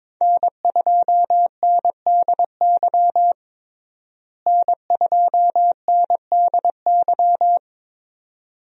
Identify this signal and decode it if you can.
Morse code, heard as a keyed tone, about 22 wpm, 700 Hz. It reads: N2NDY N2NDY
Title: USA Call Signs - 5 Characters - Repeated 2x - 22wpm